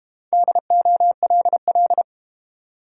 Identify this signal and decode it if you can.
Morse code, heard as a keyed tone, about 31 wpm, 700 Hz. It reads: DOLL